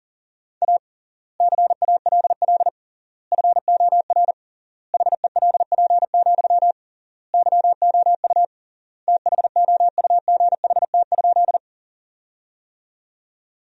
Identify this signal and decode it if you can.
Morse code, heard as a keyed tone, about 40 wpm, 700 Hz. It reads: A CALL FOR HELP, YOU THOUGHT?
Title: 3x Word-Spacing - Sentences from Top 300 Words - 40wpm